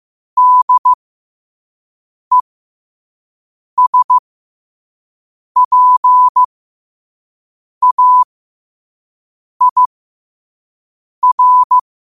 Morse code, heard as a keyed tone, about 15 wpm, spaced as for 5 wpm, 1000 Hz. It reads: DESPAIR
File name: qotdfiles / 5wpm